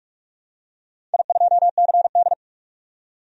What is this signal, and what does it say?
Morse code, heard as a keyed tone, about 46 wpm, 690 Hz. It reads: I2CD
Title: DXCC-100 Call Signs - 4 Characters - 45wpm